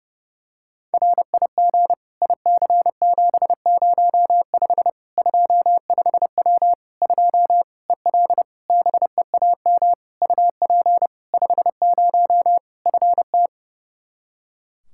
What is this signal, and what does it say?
Morse code, heard as a keyed tone, about 30 wpm, 700 Hz. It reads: RIG IC705 25W 2 EL BEAM UP 50 FT
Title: Practice QSO - Exchange 2 - Info - 30wpm